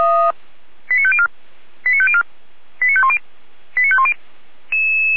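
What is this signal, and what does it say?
Five-tone selective call (ZVEI), heard as quick runs of five tones, tones of about 70 ms. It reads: 89504 89504 89510 89510
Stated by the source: ZVEI five-tone call received on SDR.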